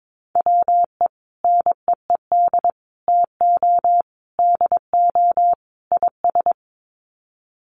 Morse code, heard as a keyed tone, about 22 wpm, 700 Hz. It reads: WE NEED TO DO IS
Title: Sets of 5 Words Encore2 22wpm